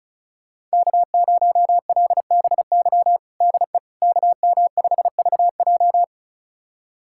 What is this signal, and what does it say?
Morse code, heard as a keyed tone, about 36 wpm, 700 Hz. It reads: K0LBY DE KM5VJ
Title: USA Callsign DE Callsign - 5 Characters - 35wpm